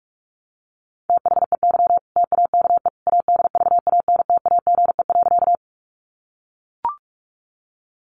Morse code, heard as a keyed tone, about 44 wpm, 700 Hz. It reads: THEY TAKE ADVANTAGE.
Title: Sentences from Top 900 Words 45wpm